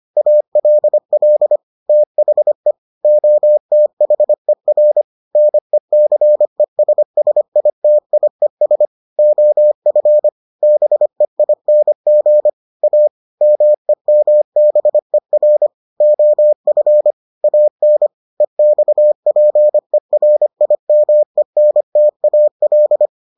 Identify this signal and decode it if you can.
Morse code, heard as a keyed tone, about 25 wpm, 600 Hz. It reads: ALL THE OTHER NECESSITIES OF BEING A MEMBER OF AN EXPERIMENTAL